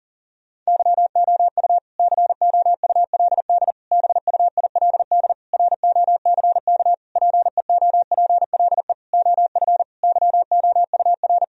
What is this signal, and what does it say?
Morse code, heard as a keyed tone, about 40 wpm, 700 Hz. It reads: YOU COULD BUILD ROCK PEOPLE OF YOUR
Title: Sentences from Top 1000 Words - Review Part 2 - 40wpm